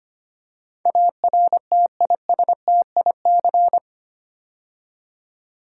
Morse code, heard as a keyed tone, about 25 wpm, 700 Hz. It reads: ARTISTIC
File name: 8 Letter Words - First 3 Letters Given - 25wpm